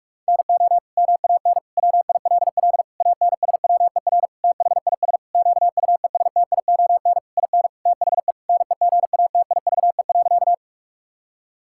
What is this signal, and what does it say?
Morse code, heard as a keyed tone, about 41 wpm, 700 Hz. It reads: NO MAN WILL ANSWER THIS QUESTION IN THE NEGATIVE.
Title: Sentences from Top 1000 Words - Review Part 6 - 45wpm